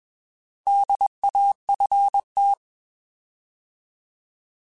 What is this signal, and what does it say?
Morse code, heard as a keyed tone, about 21 wpm, 800 Hz. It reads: DAFT
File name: Alphabetical / Alphabetical mp3s / Ds @20